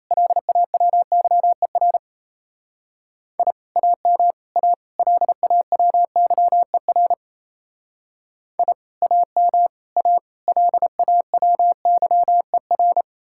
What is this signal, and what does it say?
Morse code, heard as a keyed tone, about 35 wpm, 700 Hz. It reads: LAWYER I AM A LAWYER I AM A LAWYER